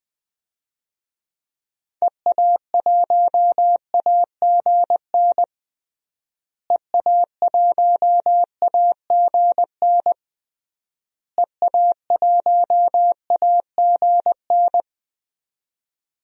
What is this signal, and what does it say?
Morse code, heard as a keyed tone, about 21 wpm, 700 Hz. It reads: EA1AGN EA1AGN EA1AGN
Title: DXCC-100 Call Signs - 6 Characters - Repeated 3x - 20wpm